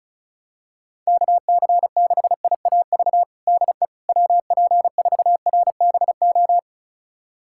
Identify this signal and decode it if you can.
Morse code, heard as a keyed tone, about 36 wpm, 700 Hz. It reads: KC6IAV DE WP4RBO
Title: USA Callsign DE Callsign 35wpm